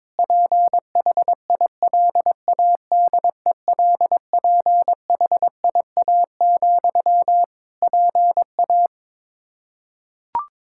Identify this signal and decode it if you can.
Morse code, heard as a keyed tone, about 22 wpm, 700 Hz. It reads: PHILADELPHIA, PA
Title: Practice QSO - Exchange 1 - Beginning - 22wpm